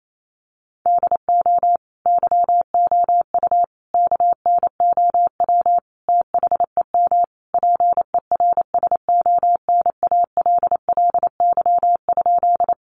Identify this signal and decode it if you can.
Morse code, heard as a keyed tone, about 28 wpm, 700 Hz. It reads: DO YOU KNOW THEM PERSONALLY?
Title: Sentences from Top 1400 Words 28wpm